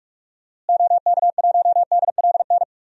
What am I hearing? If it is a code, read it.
Morse code, heard as a keyed tone, about 41 wpm, 700 Hz. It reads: OK1DLN
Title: DXCC-100 Call Signs - 6 Characters - Repeated 3x - 45wpm